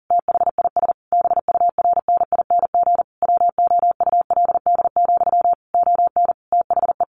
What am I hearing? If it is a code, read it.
Morse code, heard as a keyed tone, about 40 wpm, 700 Hz. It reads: THIS BURNING WOULD, ON THE
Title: A Sound Of Thunder 40wpm